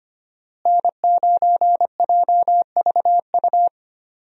Morse code, heard as a keyed tone, about 25 wpm, 700 Hz. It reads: N9JVU
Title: USA Call Signs - 5 Characters - Repeated 2x - 25wpm